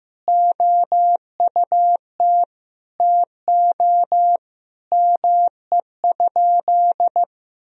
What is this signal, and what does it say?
Morse code, heard as a keyed tone, about 15 wpm, 700 Hz. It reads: OUT TO ME?
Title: Sentences from Top 200 Words and Abbreviations 15wpm